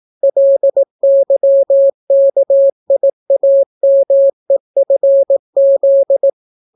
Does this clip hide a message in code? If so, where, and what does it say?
Yes: Morse code from 0.2 to 6.3 s, heard as a keyed tone, about 18 wpm, 550 Hz. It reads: LYKIAMEFZ